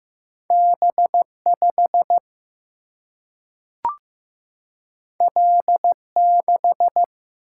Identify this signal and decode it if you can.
Morse code, heard as a keyed tone, about 15 wpm, 700 Hz. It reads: B 5 L 6
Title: Single-Letter Single-Number 15wpm